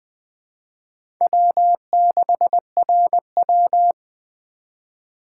Morse code, heard as a keyed tone, about 20 wpm, 700 Hz. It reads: W6RW
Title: USA Call Signs - 4 Characters - 2x Character Spacing - 20wpm